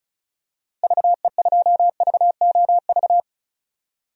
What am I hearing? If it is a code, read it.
Morse code, heard as a keyed tone, about 36 wpm, 700 Hz. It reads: VE2VOV
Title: DXCC-100 Call Signs - 6 Characters - Repeated 3x - 35wpm